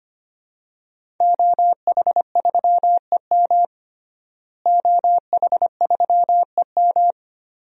Morse code, heard as a keyed tone, about 25 wpm, 700 Hz. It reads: OH3EM OH3EM
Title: DXCC-100 Call Signs - 5 Characters - Repeated 3x - 25wpm